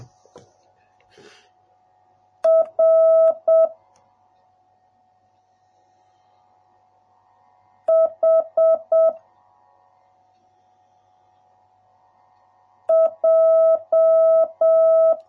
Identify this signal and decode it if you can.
Morse code, heard as a keyed tone, about 7 wpm, 650 Hz. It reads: R H J